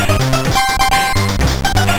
There is no code.